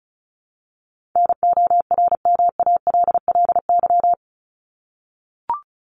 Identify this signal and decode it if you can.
Morse code, heard as a keyed tone, about 34 wpm, 700 Hz. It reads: NORMALLY
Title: Top 751-1000 Words 35wpm